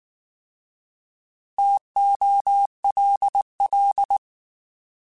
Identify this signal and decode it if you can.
Morse code, heard as a keyed tone, about 19 wpm, 800 Hz. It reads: TOLL